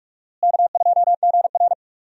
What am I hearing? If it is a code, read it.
Morse code, heard as a keyed tone, about 44 wpm, 700 Hz. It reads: K2GR
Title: USA Call Signs - 4 Characters - Repeated 2x - 45wpm